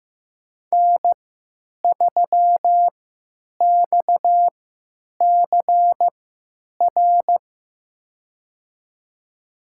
Morse code, heard as a keyed tone, about 15 wpm, spaced as for 9 wpm, 700 Hz. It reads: N3XCR
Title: USA Call Signs - 5 Characters - 3x Character Spacing - 15wpm